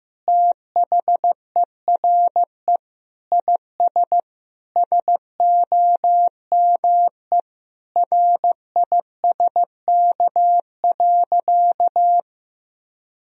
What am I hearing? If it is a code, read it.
Morse code, heard as a keyed tone, about 15 wpm, 700 Hz. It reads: THERE IS SOME RISK.
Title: Sentences from Top 1000 Words - Review Part 4 - 15wpm